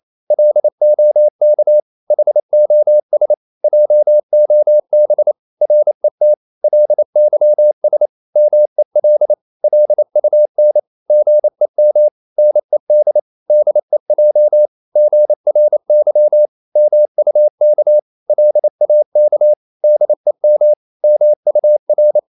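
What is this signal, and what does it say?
Morse code, heard as a keyed tone, about 28 wpm, 600 Hz. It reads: LOK HOS JOB RET LYS MEL LUN GEM NED DEJ GRY MUK LAK DEM MUR
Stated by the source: Danish short words sent in Morse at 28 wpm, from CW med Gnister.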